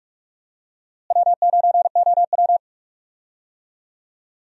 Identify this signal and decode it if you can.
Morse code, heard as a keyed tone, about 46 wpm, 700 Hz. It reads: W9OW
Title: USA Call Signs - 4 Characters - Repeated 2x - 45wpm